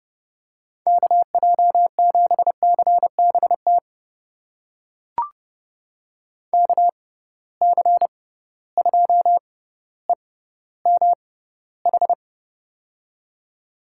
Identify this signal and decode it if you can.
Morse code, heard as a keyed tone, about 30 wpm, spaced as for 15 wpm, 700 Hz. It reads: KJ7CBT KC2EMH
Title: USA Call Signs - 6 Characters - 6x Character Spacing - 30wpm